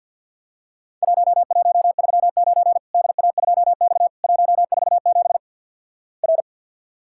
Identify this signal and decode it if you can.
Morse code, heard as a keyed tone, about 50 wpm, 700 Hz. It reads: 1139 DA2X 147 R
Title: Worked All Europe DX Contest - QTC Exchange with Repeats - 50wpm